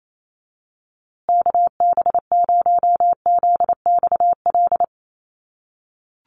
Morse code, heard as a keyed tone, about 28 wpm, 700 Hz. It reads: KB0ZXL
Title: USA Call Signs Repeated 2x 28wpm